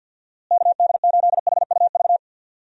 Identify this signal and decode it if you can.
Morse code, heard as a keyed tone, about 51 wpm, 700 Hz. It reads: KD8HUV
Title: USA Call Signs - 6 Characters - Repeated 2x - 50wpm